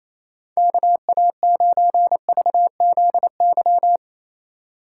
Morse code, heard as a keyed tone, about 36 wpm, 700 Hz. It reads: KA9VZY